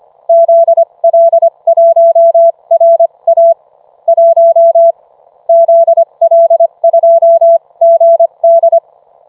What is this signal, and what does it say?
Morse code, heard as a keyed tone, about 25 wpm, spaced as for 22 wpm, 670 Hz. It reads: ZL1RA 1 ZL2GD